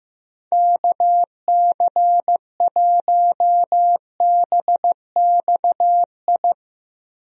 Morse code, heard as a keyed tone, about 15 wpm, 700 Hz. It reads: KC1BXI
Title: USA Call Signs - 6 Characters - Repeated 2x - 15wpm